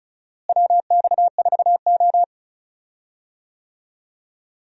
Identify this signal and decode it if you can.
Morse code, heard as a keyed tone, about 34 wpm, 700 Hz. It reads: WX4O
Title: USA Call Signs - 4 Characters - Repeated 2x - 35wpm